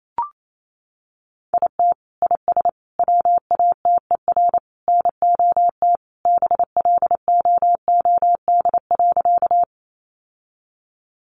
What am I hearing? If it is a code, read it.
Morse code, heard as a keyed tone, about 28 wpm, 700 Hz. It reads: IT IS WATER NOT BLOOD.
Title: Sentences from Top 1100 Words 28wpm